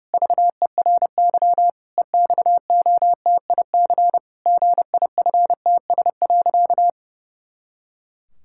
Morse code, heard as a keyed tone, about 32 wpm, 700 Hz. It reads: VERY EXOTIC GIFTS.
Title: Pangram Challenge v2 - 30wpm